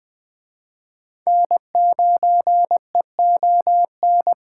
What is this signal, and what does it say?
Morse code, heard as a keyed tone, about 20 wpm, 700 Hz. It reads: N9EON
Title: Speed Racing - USA 1x3 Call Signs 17wpm